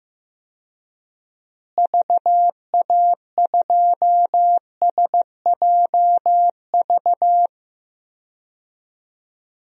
Morse code, heard as a keyed tone, about 15 wpm, 700 Hz. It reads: VA2SJV
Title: DXCC-100 Call Signs - 6 Characters - Repeated 2x - 15wpm